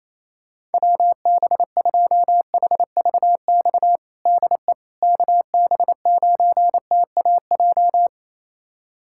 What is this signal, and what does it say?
Morse code, heard as a keyed tone, about 28 wpm, 700 Hz. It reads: WB2HVX DE KB9TAJ
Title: USA Callsign DE Callsign - 6 Characters - 28wpm